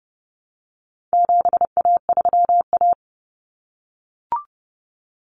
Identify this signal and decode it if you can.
Morse code, heard as a keyed tone, about 30 wpm, 700 Hz. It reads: 7A3A
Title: Common Cut Numbers - 4-Digit 30wpm